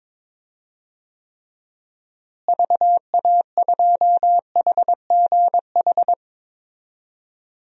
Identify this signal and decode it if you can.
Morse code, heard as a keyed tone, about 22 wpm, 700 Hz. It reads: VA2HGH
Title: DXCC-100 Call Signs - 6 Characters - Repeated 2x - 22wpm